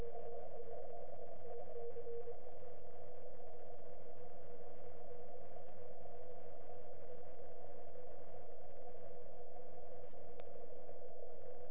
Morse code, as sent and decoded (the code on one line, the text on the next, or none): none